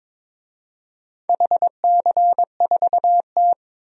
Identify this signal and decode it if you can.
Morse code, heard as a keyed tone, about 22 wpm, 700 Hz. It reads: HC4T
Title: DXCC-100 Call Signs - 4 Characters - 22wpm